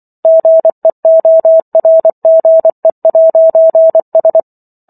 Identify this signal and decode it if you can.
Morse code, heard as a keyed tone, about 24 wpm, 650 Hz. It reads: GEORGE'S